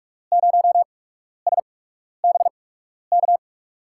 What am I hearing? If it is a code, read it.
Morse code, heard as a keyed tone, about 46 wpm, spaced as for 12 wpm, 700 Hz. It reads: 0SBK
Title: USA Call Signs - 6 Characters - 8x Character Spacing - 45wpm